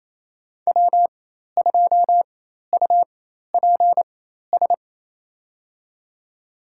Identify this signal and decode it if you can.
Morse code, heard as a keyed tone, about 28 wpm, spaced as for 13 wpm, 700 Hz. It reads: W2UPS
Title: USA Call Signs - 4x Character Spacing - 28wpm